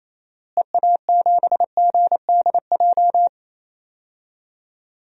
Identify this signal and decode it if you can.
Morse code, heard as a keyed tone, about 28 wpm, 700 Hz. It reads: EA7GDJ
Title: DXCC-100 Call Signs - 6 Characters - Repeated 3x - 28wpm